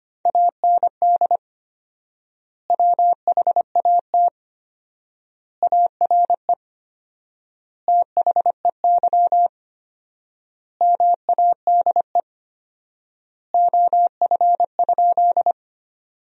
Morse code, heard as a keyed tone, about 25 wpm, 700 Hz. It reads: AND WHAT ARE THEY MADE OF?
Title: 4x Word-Spacing - Sentences from Top 200 Words - 25wpm